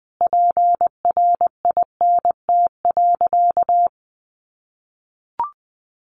Morse code, heard as a keyed tone, about 20 wpm, 700 Hz. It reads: PRINT.
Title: Sentences from Top 1400 Words 20wpm